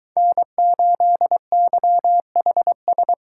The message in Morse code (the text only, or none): N8YHS